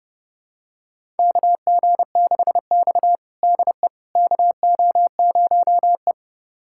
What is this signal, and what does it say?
Morse code, heard as a keyed tone, about 30 wpm, 700 Hz. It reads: KG6X DE KO0E
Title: USA Callsign DE Callsign - 4 Characters - 30wpm